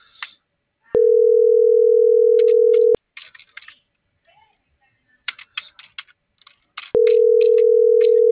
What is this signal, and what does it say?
Ringback tone (North America), 440 + 480 Hz.